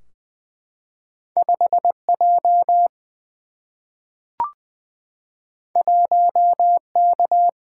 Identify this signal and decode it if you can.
Morse code, heard as a keyed tone, about 20 wpm, 700 Hz. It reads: 5J 1K
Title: Single-Letter Single-Number 20wpm